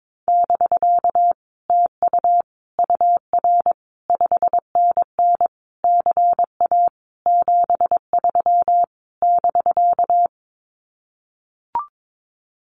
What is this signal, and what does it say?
Morse code, heard as a keyed tone, about 22 wpm, 700 Hz. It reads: <BK> TU UR 5NN CA 73 <BK>
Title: Common QSO Phrases Encore 22wpm